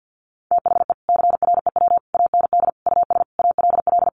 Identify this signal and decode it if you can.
Morse code, heard as a keyed tone, about 50 wpm, 700 Hz. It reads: THE CREW AND US ALL
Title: Sentences from Top 1500 Words 50wpm